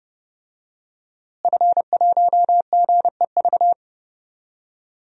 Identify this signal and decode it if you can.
Morse code, heard as a keyed tone, about 31 wpm, 700 Hz. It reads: F1GEV
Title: DXCC-100 Call Signs - 5 Characters - Repeated 3x - 30wpm